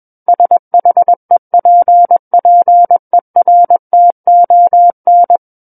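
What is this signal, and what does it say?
Morse code, heard as a keyed tone, about 21 wpm, 700 Hz. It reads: SHEPPERTON
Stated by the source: Morse code practice reading of a book chapter at 21 words per minute.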